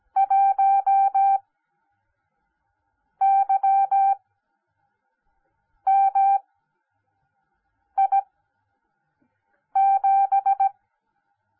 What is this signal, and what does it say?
Morse code, heard as a keyed tone, about 17 wpm, spaced as for 5 wpm, 780 Hz. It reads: 1YMI7